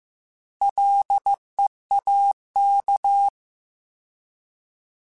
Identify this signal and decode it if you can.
Morse code, heard as a keyed tone, about 15 wpm, 800 Hz. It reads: LEAK
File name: Combinations / Combinations-mp3s / Combinations@13